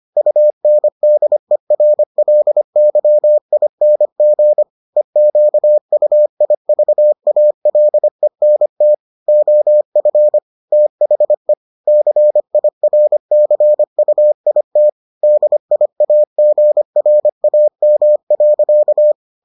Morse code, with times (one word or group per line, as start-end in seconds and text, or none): 0.2-4.6 UNDERLYING
5.0-8.9 EQUIVALENT
9.3-10.4 OF
10.7-11.5 THE
11.9-14.9 CIRCUIT
15.2-19.1 DIAGRAM.